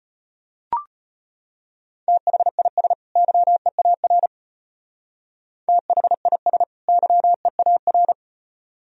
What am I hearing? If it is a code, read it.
Morse code, heard as a keyed tone, about 32 wpm, 700 Hz. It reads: THIS YEAR THIS YEAR